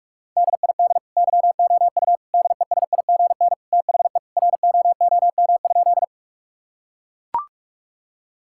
Morse code, heard as a keyed tone, about 41 wpm, 700 Hz. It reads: DID YOU DESIGN THE ROOM?
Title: Sentences from Top 1000 Words - Review Part 5 - 45wpm